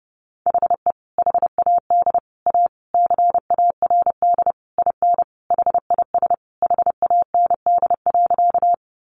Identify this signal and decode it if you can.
Morse code, heard as a keyed tone, about 30 wpm, 700 Hz. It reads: HE HAD A CARD IN HIS HAND.
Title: Sentences from Top 600 Words 30wpm